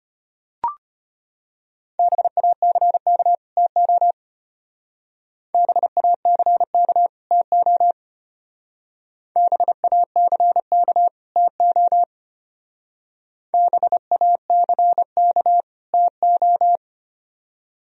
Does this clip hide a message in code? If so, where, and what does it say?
Yes: Morse code from 2.0 to 16.8 s, heard as a keyed tone, about 32 wpm, 700 Hz. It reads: BACK TO BACK TO BACK TO BACK TO